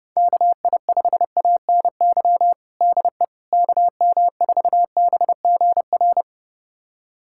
Morse code, heard as a keyed tone, about 32 wpm, 700 Hz. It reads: KI5ANY DE KM4BGR
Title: USA Callsign DE Callsign 30wpm